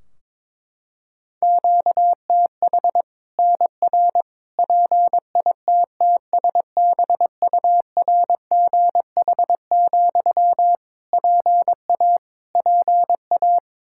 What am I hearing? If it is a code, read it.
Morse code, heard as a keyed tone, about 22 wpm, 700 Hz. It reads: QTH NR PITTSBURGH, PA PA
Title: Practice QSO - Exchange 1 - Beginning - 22wpm